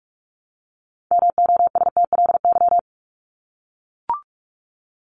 Morse code, heard as a keyed tone, about 44 wpm, 700 Hz. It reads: MOSTLY
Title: Top 1001-1250 Words 45wpm